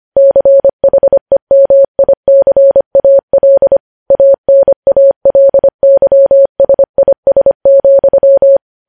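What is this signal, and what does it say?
Morse code, heard as a keyed tone, about 25 wpm, 550 Hz. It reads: CHEMICAL ANALYSIS,
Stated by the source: Morse code news headlines sent at 25 wpm.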